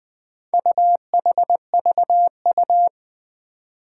Morse code, heard as a keyed tone, about 20 wpm, 700 Hz. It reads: UHVU
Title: Four Challenging Letter-Number Combinations 20wpm